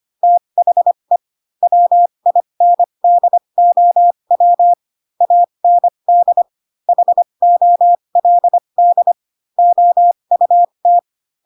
Morse code, heard as a keyed tone, about 25 wpm, spaced as for 22 wpm, 700 Hz. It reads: THE WINDOW AND HOLD OUT
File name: Pinocchio / Pinocchio ch6 18 25 ik1hge